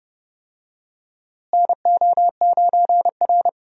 Morse code, heard as a keyed tone, about 31 wpm, 700 Hz. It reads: NO9R
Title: USA Call Signs - 4 Characters - Repeated 2x - 30wpm